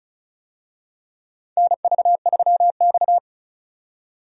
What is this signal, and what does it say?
Morse code, heard as a keyed tone, about 34 wpm, 700 Hz. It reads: NV3X
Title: USA Call Signs - 4 Characters - 35wpm